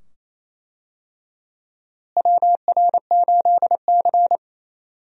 Morse code, heard as a keyed tone, about 28 wpm, 700 Hz. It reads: WR8C
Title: USA Call Signs - 4 Characters - Repeated 2x - 28wpm